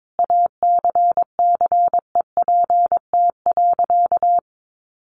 Morse code, heard as a keyed tone, about 22 wpm, 700 Hz. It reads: ACCEPT.
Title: Sentences from Top 1300 Words 22wpm